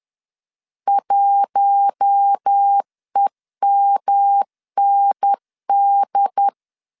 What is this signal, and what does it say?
Morse code, heard as a keyed tone, about 10 wpm, 800 Hz. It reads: 1EMND